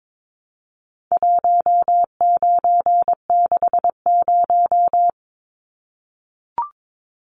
Morse code, heard as a keyed tone, about 22 wpm, 700 Hz. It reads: 1960
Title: Numbers - 4-Digit 22wpm